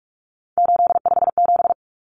Morse code, heard as a keyed tone, about 44 wpm, 700 Hz. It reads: 857
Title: Numbers - 3-Digit 45wpm